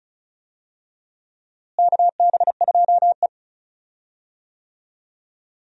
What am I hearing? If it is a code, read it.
Morse code, heard as a keyed tone, about 36 wpm, 700 Hz. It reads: KB2E
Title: USA Call Signs - 4 Characters - Repeated 3x - 35wpm